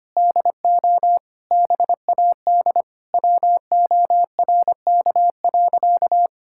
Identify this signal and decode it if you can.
Morse code, heard as a keyed tone, about 25 wpm, 700 Hz. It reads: DO BAD WORK.
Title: Sentences from Top 1000 Words - Review Part 6 - 25wpm